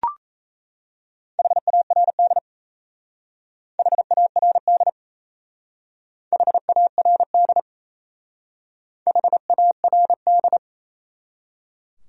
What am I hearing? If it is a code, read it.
Morse code, heard as a keyed tone, about 33 wpm, 700 Hz. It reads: HARD HARD HARD HARD
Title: Speed-Racing Top 500 Words 28wpm